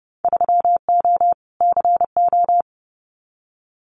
Morse code, heard as a keyed tone, about 31 wpm, 700 Hz. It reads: 3O CO
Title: Winter Field Day Exchanges Encore 30wpm